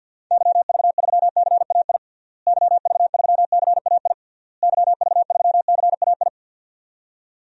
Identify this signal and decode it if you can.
Morse code, heard as a keyed tone, about 49 wpm, 700 Hz. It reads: YV3CAI YV3CAI YV3CAI
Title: DXCC-100 Call Signs - 6 Characters - Repeated 3x - 50wpm